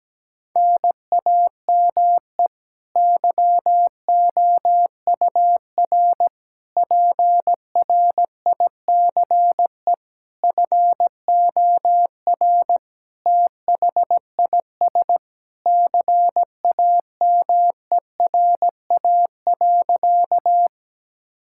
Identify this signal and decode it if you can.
Morse code, heard as a keyed tone, about 17 wpm, 700 Hz. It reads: NAME YOUR PRICE FOR THIS CAMERA.
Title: Sentences from Top 1000 Words - Review Part 5 - 17wpm